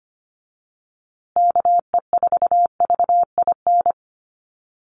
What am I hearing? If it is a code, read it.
Morse code, heard as a keyed tone, about 25 wpm, 700 Hz. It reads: KE4VIN
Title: USA Call Signs Repeated 3x 25wpm